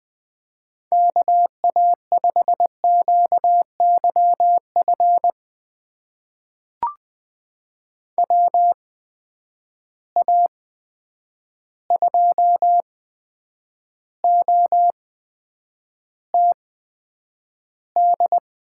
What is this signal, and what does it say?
Morse code, heard as a keyed tone, about 20 wpm, spaced as for 9 wpm, 700 Hz. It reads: KA5QYF WA2OTD
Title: USA Call Signs - 6 Characters - 8x Character Spacing - 20wpm